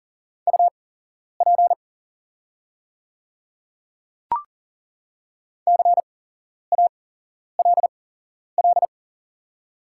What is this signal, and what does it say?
Morse code, heard as a keyed tone, about 40 wpm, spaced as for 11 wpm, 700 Hz. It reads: UP CALL
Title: 8x Character Spacing - Top 200 Words - 40wpm